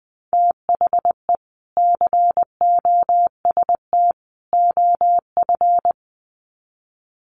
Morse code, heard as a keyed tone, about 20 wpm, 700 Hz. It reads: THE COST OF
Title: Sets of 3 Words 20wpm